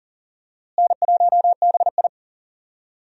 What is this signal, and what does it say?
Morse code, heard as a keyed tone, about 38 wpm, 700 Hz. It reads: N1BI
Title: USA Call Signs - 4 Characters - Repeated 2x - 40wpm